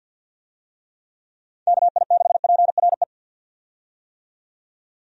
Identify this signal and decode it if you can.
Morse code, heard as a keyed tone, about 51 wpm, 700 Hz. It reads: KI6PRE
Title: USA Call Signs - 6 Characters - 50wpm